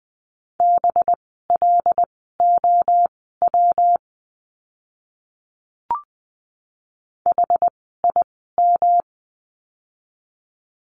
Morse code, heard as a keyed tone, about 20 wpm, spaced as for 15 wpm, 700 Hz. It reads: BLOW HIM
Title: 185 Lesson 30f - Letter b - Words 20wpm